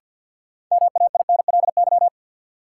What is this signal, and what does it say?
Morse code, heard as a keyed tone, about 53 wpm, 700 Hz. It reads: MAINLY
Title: Sentences from Top 1400 Words - New Word Drill - 50wpm